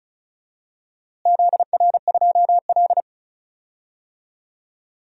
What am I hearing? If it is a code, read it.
Morse code, heard as a keyed tone, about 36 wpm, 700 Hz. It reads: ZR2L
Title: DXCC-100 Call Signs - 4 Characters - Repeated 2x - 35wpm